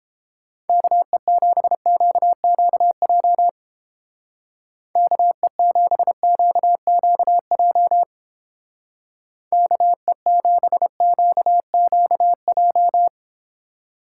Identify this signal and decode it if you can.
Morse code, heard as a keyed tone, about 30 wpm, 700 Hz. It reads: KE7QQJ KE7QQJ KE7QQJ